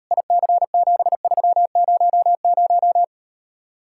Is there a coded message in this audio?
Yes: Morse code from 0.1 to 3.0 s, heard as a keyed tone, about 39 wpm, 700 Hz. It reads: IC7300